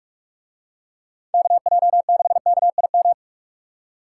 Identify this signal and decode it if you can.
Morse code, heard as a keyed tone, about 45 wpm, 700 Hz. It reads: KJ6KIM